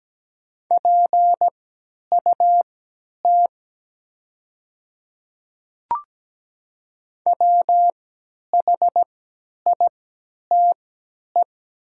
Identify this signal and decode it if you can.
Morse code, heard as a keyed tone, about 17 wpm, spaced as for 10 wpm, 700 Hz. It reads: PUT WHITE
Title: 3x Character Spacing - Top 500 Words - 17wpm